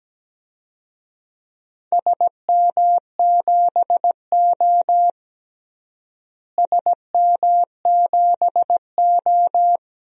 Morse code, heard as a keyed tone, about 17 wpm, 700 Hz. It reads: SM7O SM7O
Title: DXCC-100 Call Signs - 4 Characters - Repeated 3x - 17wpm